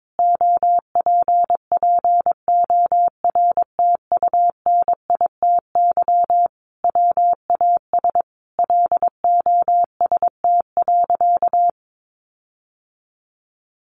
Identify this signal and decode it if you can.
Morse code, heard as a keyed tone, about 22 wpm, 700 Hz. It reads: OPPORTUNITY WAS LOST.
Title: Sentences from Top 1400 Words 22wpm